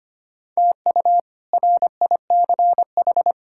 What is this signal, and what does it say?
Morse code, heard as a keyed tone, about 25 wpm, 700 Hz. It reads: TU RICH
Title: K1USN SST QSO Practice - 25wpm